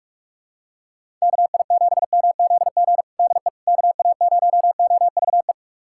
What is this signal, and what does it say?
Morse code, heard as a keyed tone, about 42 wpm, 700 Hz. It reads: KI7MZG DE KA0OVE
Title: USA Callsign DE Callsign 45wpm